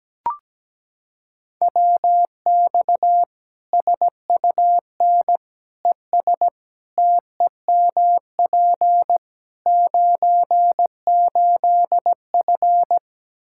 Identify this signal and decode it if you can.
Morse code, heard as a keyed tone, about 17 wpm, 700 Hz. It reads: WX SUN ES TEMP 98F
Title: K1USN SST QSO Practice 17wpm